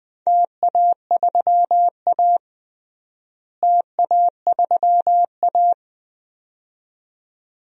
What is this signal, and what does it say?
Morse code, heard as a keyed tone, about 20 wpm, 700 Hz. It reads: TA3A TA3A
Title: DXCC-100 Call Signs - 4 Characters - Repeated 2x - 20wpm